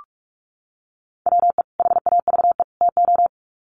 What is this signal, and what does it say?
Morse code, heard as a keyed tone, about 44 wpm, 700 Hz. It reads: WE HAVE TO
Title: Sets of 3 Words 45wpm